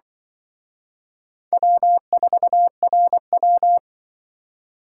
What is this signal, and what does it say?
Morse code, heard as a keyed tone, about 23 wpm, 700 Hz. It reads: W4RW
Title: Speed Racing - USA 1x2 Call Signs 20wpm